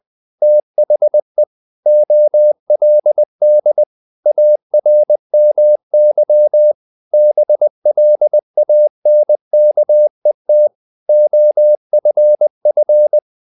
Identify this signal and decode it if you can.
Morse code, heard as a keyed tone, about 20 wpm, 600 Hz. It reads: THE OLD ARMY BLANKET OFF
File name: the old man and the sea / the old man and the sea part 1 20wpm